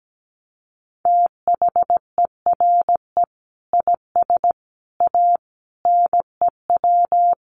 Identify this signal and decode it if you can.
Morse code, heard as a keyed tone, about 17 wpm, 700 Hz. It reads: THERE IS A NEW
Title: Sentences from Top 1100 Words 17wpm